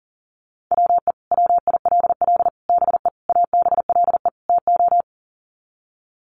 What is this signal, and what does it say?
Morse code, heard as a keyed tone, about 40 wpm, 700 Hz. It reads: WE WILL BE ABLE TO
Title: Sets of 5 Words Encore 40wpm